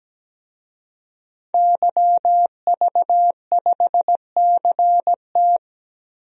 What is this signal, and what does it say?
Morse code, heard as a keyed tone, about 17 wpm, 700 Hz. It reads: YV5CT
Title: DXCC-100 Call Signs - 5 Characters - 17wpm